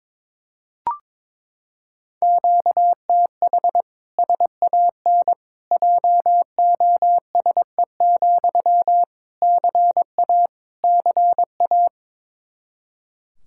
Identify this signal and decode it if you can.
Morse code, heard as a keyed tone, about 21 wpm, 700 Hz. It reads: QTH SAN JOSE, CA CA
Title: Practice QSO - Exchange 1 - Beginning - 22wpm